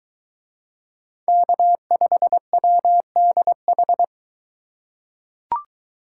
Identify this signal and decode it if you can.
Morse code, heard as a keyed tone, about 22 wpm, 700 Hz. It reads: K5WDH